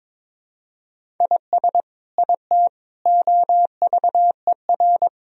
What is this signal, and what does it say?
Morse code, heard as a keyed tone, about 22 wpm, 700 Hz. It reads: IS IT OVER
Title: Sentences from Top 1000 Words - Review Part 1 - 22wpm